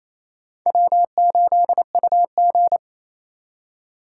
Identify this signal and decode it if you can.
Morse code, heard as a keyed tone, about 36 wpm, 700 Hz. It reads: W8UG